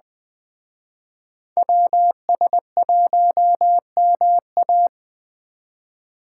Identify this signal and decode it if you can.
Morse code, heard as a keyed tone, about 21 wpm, 700 Hz. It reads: WS1MA
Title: Speed Racing - USA Call Signs 17wpm